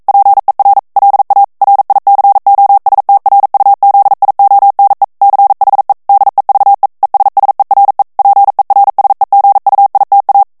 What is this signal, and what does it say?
Morse code, heard as a keyed tone, about 42 wpm, 800 Hz. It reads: PER LA RICOSTRUZIONE CHE DEVE ESSERE PERSEGUITA
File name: morse mp3 / ITALIEN texte clair 1 vitesse 40 mots minute